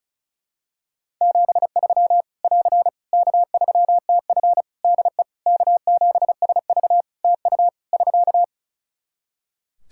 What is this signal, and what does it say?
Morse code, heard as a keyed tone, about 36 wpm, 700 Hz. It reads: 73 <AR> K3TF DE K7SV TU <SK>
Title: Practice QSO - Exchange 4 - Ending - 35wpm